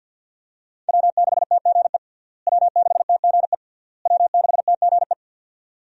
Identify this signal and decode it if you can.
Morse code, heard as a keyed tone, about 46 wpm, 700 Hz. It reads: W6TGE W6TGE W6TGE
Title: USA Call Signs - 5 Characters - Repeated 3x - 50wpm